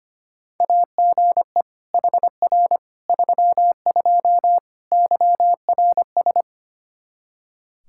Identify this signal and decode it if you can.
Morse code, heard as a keyed tone, about 25 wpm, 700 Hz. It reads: AGE HR 32 YRS
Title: Practice QSO - Exchange 3 - More Info - 25wpm